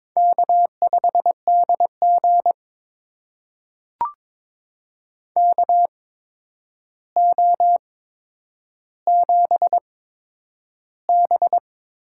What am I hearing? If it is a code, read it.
Morse code, heard as a keyed tone, about 23 wpm, spaced as for 10 wpm, 700 Hz. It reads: K5DG KO7B
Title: USA Call Signs - 4 Characters - 8x Character Spacing - 22wpm